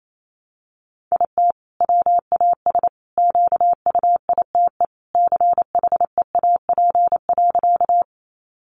Morse code, heard as a keyed tone, about 28 wpm, 700 Hz. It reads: IT WAS QUITE CHEAP.
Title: Sentences from Top 1200 Words 28wpm